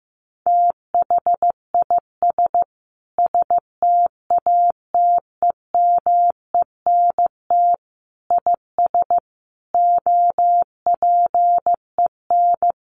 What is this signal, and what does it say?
Morse code, heard as a keyed tone, about 15 wpm, 700 Hz. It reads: THIS STATEMENT IS OPEN
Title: Sentences from Top 1300 Words 15wpm